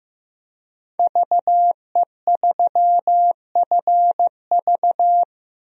Morse code, heard as a keyed tone, about 15 wpm, 700 Hz. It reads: VE3FV
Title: DXCC-100 Call Signs - 5 Characters - Repeated 3x - 15wpm